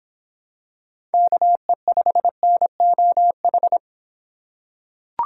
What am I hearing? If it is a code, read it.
Morse code, heard as a keyed tone, about 26 wpm, 700 Hz. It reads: KE5NOH